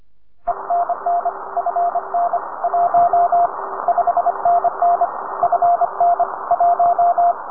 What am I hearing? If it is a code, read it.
Morse code, heard as a keyed tone, about 25 wpm, 700 Hz, with radio noise. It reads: ENN FN 1 5NN FN 1
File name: attachments / UA1CEG part